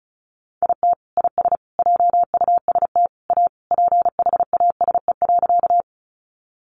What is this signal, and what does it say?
Morse code, heard as a keyed tone, about 34 wpm, 700 Hz. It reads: IT IS JUST A PHASE.
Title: Sentences from Top 1200 Words 35wpm